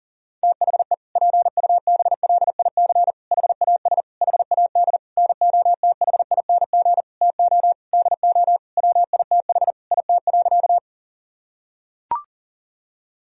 Morse code, heard as a keyed tone, about 40 wpm, 700 Hz. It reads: THE PUBLIC HAS HAD NOTHING TO DO WITH IT.
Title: Sentences from Top 1000 Words - Review Part 3 - 40wpm